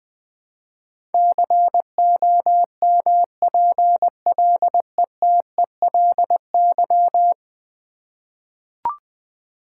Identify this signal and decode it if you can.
Morse code, heard as a keyed tone, about 20 wpm, 700 Hz. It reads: COMPLETELY
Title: Ten Letter Words - 20wpm